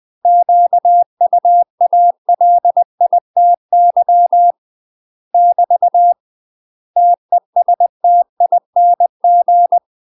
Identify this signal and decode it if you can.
Morse code, heard as a keyed tone, about 20 wpm, 700 Hz. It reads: QUALITY <BT> TESTING
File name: UK MASTER QSOs TRAINING / 20WPM / UK MASTER QSOs 0060